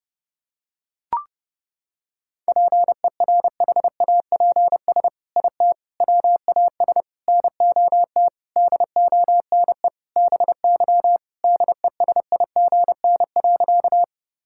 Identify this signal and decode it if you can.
Morse code, heard as a keyed tone, about 30 wpm, 700 Hz. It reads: PERHAPS IT WAS NOT DONE BY DESIGN.
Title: Sentences from Top 1000 Words - Review Part 5 - 30wpm